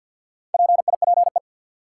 Morse code, heard as a keyed tone, about 48 wpm, 700 Hz. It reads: PIPE
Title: Sentences from Top 1500 Words - New Word Drill Alt - 50wpm